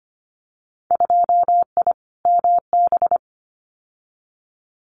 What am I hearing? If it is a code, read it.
Morse code, heard as a keyed tone, about 25 wpm, 700 Hz. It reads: 2I MB
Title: Winter Field Day Exchanges Encore 25wpm